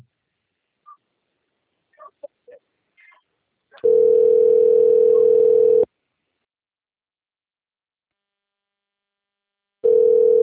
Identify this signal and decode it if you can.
Ringback tone (North America), 440 + 480 Hz.
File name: rg-1600-unknown- - - .56989